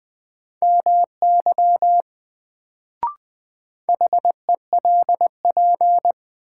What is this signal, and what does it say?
Morse code, heard as a keyed tone, about 20 wpm, 700 Hz. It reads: MY HELP
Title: Sets of 2 Words - New Word Drill - 20wpm